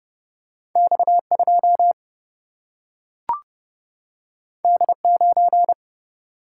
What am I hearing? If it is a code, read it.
Morse code, heard as a keyed tone, about 30 wpm, 700 Hz. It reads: X2 D9
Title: Single-Letter Single-Number 30wpm